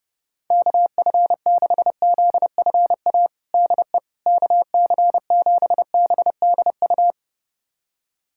Morse code, heard as a keyed tone, about 30 wpm, 700 Hz. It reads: KF6ZFA DE KC7BDU
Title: USA Callsign DE Callsign - 6 Characters - 30wpm